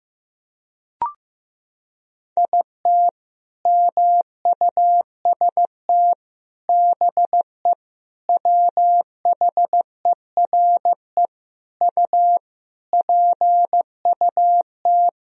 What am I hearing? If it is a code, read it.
Morse code, heard as a keyed tone, about 16 wpm, 700 Hz. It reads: IT MUST BE WHERE U PUT
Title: Sentences from Top 200 Words and Abbreviations 15wpm